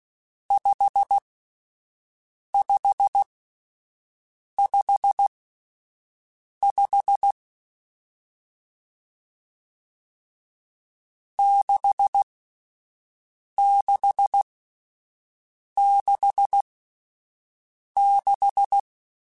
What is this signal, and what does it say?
Morse code, heard as a keyed tone, about 16 wpm, spaced as for 5 wpm, 800 Hz. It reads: 5555 6666